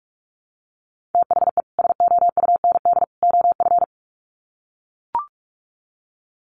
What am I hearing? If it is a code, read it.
Morse code, heard as a keyed tone, about 44 wpm, 700 Hz. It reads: THE SOUND OF
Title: Sets of 3 Words 45wpm